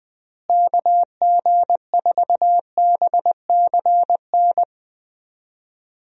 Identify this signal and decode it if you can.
Morse code, heard as a keyed tone, about 20 wpm, 700 Hz. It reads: KG4BCN
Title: USA Call Signs - 6 Characters - Repeated 3x - 20wpm